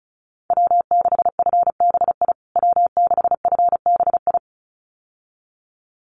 Morse code, heard as a keyed tone, about 34 wpm, 700 Hz. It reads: W6FBI W6FBI
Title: USA Call Signs Repeated 2x 35wpm